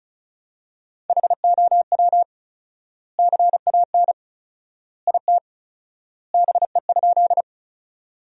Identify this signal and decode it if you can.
Morse code, heard as a keyed tone, about 36 wpm, 700 Hz. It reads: HOW CAN IT BE?
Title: 4x Word-Spacing - Sentences from Top 100 Words - 35wpm